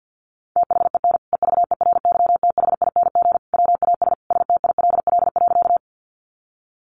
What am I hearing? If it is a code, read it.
Morse code, heard as a keyed tone, about 50 wpm, 700 Hz. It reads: THEN EVERYTHING WAS STILL.
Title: Sentences from Top 700 Words 50wpm